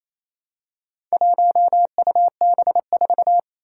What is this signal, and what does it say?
Morse code, heard as a keyed tone, about 29 wpm, 700 Hz. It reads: 1UB4
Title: Four Challenging Letter-Number Combinations 28wpm